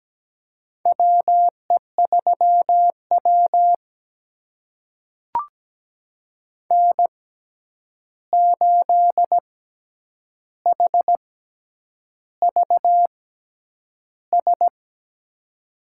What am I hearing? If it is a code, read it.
Morse code, heard as a keyed tone, about 17 wpm, spaced as for 8 wpm, 700 Hz. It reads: WE3W N8HVS
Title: USA Call Signs - 6x Character Spacing - 17wpm